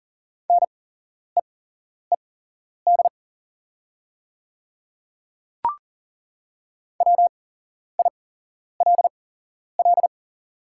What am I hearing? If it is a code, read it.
Morse code, heard as a keyed tone, about 41 wpm, spaced as for 11 wpm, 690 Hz. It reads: NEED WILL
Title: 8x Character Spacing - Top 200 Words - 40wpm